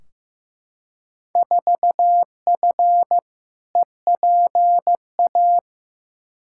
Morse code, heard as a keyed tone, about 15 wpm, 700 Hz. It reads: 4F EPA
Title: Field Day Exchanges Encore - 15wpm